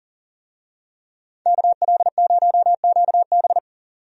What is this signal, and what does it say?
Morse code, heard as a keyed tone, about 39 wpm, 700 Hz. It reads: KL0QB
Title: USA Call Signs - 5 Characters - 40wpm